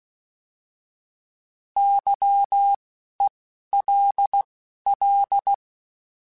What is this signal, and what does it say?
Morse code, heard as a keyed tone, about 16 wpm, 800 Hz. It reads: Y E L L
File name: Misc / Misc-mp3s / Misc @10